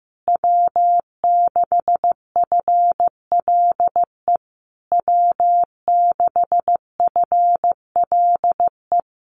Morse code, heard as a keyed tone, about 15 wpm, 700 Hz. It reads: W6FLE W6FLE
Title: Common QSO Phrases Encore 15wpm